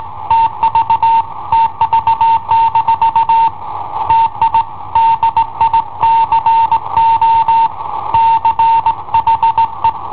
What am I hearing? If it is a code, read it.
Morse code, heard as a keyed tone, about 18 wpm, 910 Hz. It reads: TV TV- TI DICO CHE
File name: i7ohp / cassetta i7ohp B